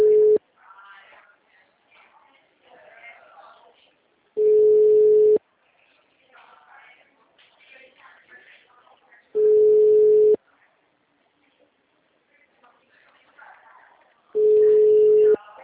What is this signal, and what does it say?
Ringback tone (Europe, 425 Hz), with speech.
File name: out- -2012- - - .94445